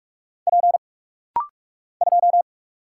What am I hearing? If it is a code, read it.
Morse code, heard as a keyed tone, about 44 wpm, 700 Hz. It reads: P 2
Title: Single Letter-Number Alt - Rapid-Fire 45wpm